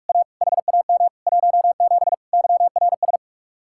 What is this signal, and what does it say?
Morse code, heard as a keyed tone, about 49 wpm, 700 Hz. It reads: A HAM 17 YRS